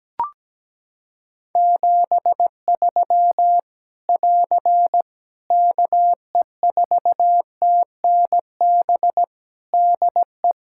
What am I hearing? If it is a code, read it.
Morse code, heard as a keyed tone, about 17 wpm, 700 Hz. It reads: 73 <AR> KE4TNB DE
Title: K1USN SST QSO Practice 17wpm